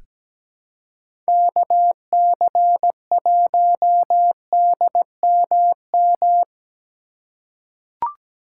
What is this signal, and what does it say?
Morse code, heard as a keyed tone, about 17 wpm, 700 Hz. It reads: KC1DMM
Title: USA Call Signs - 6 Characters - 3x Character Spacing - 17wpm